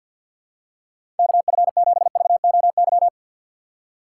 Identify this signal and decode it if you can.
Morse code, heard as a keyed tone, about 53 wpm, 700 Hz. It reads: KF7VOY
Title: USA Call Signs - 6 Characters - 6x Character Spacing - 50wpm